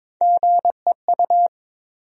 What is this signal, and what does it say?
Morse code, heard as a keyed tone, about 22 wpm, 700 Hz. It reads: GEU
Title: USA Call Sign Postfixes - 3 Letters 22wpm